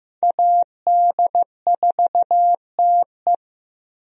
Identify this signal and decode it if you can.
Morse code, heard as a keyed tone, about 15 wpm, 700 Hz. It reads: AD4TE
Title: USA Call Signs - 5 Characters - Repeated 2x - 15wpm